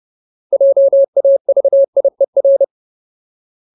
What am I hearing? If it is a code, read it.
Morse code, heard as a keyed tone, about 30 wpm, 550 Hz. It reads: JAVIER
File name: Ultimate Word List / 30WPM Ultimate